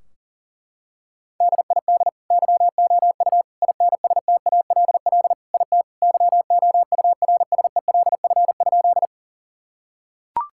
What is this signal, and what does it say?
Morse code, heard as a keyed tone, about 37 wpm, 700 Hz. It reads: DID YOU INSTALL IT YOURSELF?
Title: Sentences from Top 1600 Words 40wpm